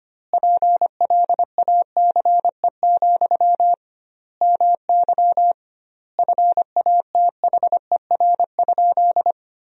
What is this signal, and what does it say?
Morse code, heard as a keyed tone, about 25 wpm, 700 Hz. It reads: PLACE, MY FATHER?
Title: 2x Word-Spacing - Sentences from Top 400 Words - 25wpm